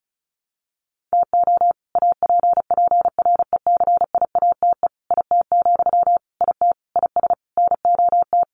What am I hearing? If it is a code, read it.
Morse code, heard as a keyed tone, about 34 wpm, 700 Hz. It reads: TO APPRECIATE IT, IT IS NOT
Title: Sentences from Top 1300 Words 35wpm